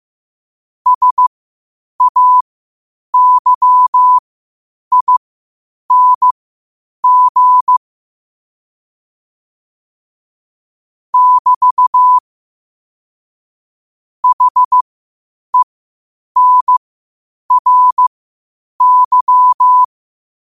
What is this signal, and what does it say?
Morse code, heard as a keyed tone, about 15 wpm, spaced as for 8 wpm, 1000 Hz. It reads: SAYING <BT> HENRY
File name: qotdfiles / 8wpm